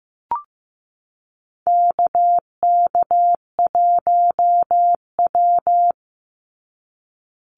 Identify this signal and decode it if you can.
Morse code, heard as a keyed tone, about 15 wpm, 700 Hz. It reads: KK1W
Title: NA SOTA Activator Callsigns 15wpm